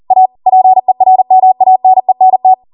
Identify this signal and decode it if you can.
Morse code, heard as a keyed tone, about 40 wpm, 750 Hz. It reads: A PERMANENT
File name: Morse / 40WPM